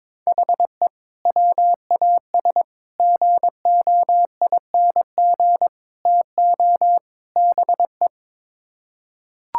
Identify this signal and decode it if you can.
Morse code, heard as a keyed tone, about 22 wpm, 700 Hz. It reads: HE WAS GOING TO BE
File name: practice / 22wpm / Build A Phrase - 5-Word Phrases - 22wpm